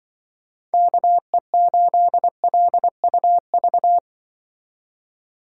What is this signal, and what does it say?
Morse code, heard as a keyed tone, about 25 wpm, 700 Hz. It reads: KE8LUV
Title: Speed Racing - USA 2x3 Call Signs 20wpm